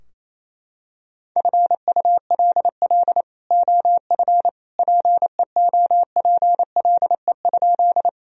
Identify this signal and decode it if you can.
Morse code, heard as a keyed tone, about 28 wpm, 700 Hz. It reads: FULL OF PEOPLE?
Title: Sentences from Top 1000 Words - Review Part 3 - 28wpm